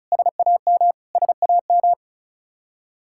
Morse code, heard as a keyed tone, about 34 wpm, 700 Hz. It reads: SAM SAM
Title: Practice QSO - Exchange 1 - Beginning - 35wpm